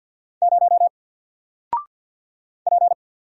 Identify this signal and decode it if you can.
Morse code, heard as a keyed tone, about 48 wpm, 700 Hz. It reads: 0 P
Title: Single Character (Letter:Number:Punctuation:Prosign) Alt 50wpm